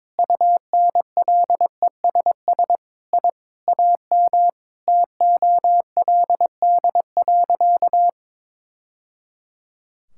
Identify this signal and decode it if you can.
Morse code, heard as a keyed tone, about 22 wpm, 700 Hz. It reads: UNLESS I AM TOLD.
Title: Sentences from Top 1000 Words - Review Part 6 - 22wpm